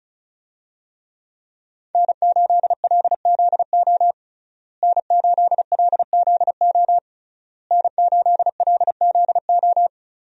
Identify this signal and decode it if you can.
Morse code, heard as a keyed tone, about 36 wpm, 700 Hz. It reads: N8LZO N8LZO N8LZO
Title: USA Call Signs - 5 Characters - Repeated 3x - 35wpm